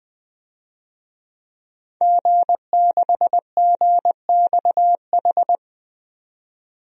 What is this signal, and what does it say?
Morse code, heard as a keyed tone, about 21 wpm, 700 Hz. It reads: G6GXH
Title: DXCC-100 Call Signs - 5 Characters - 20wpm